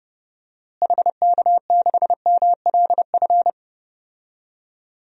Morse code, heard as a keyed tone, about 31 wpm, 700 Hz. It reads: HK6MLF
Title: DXCC-100 Call Signs - 6 Characters - Repeated 2x - 30wpm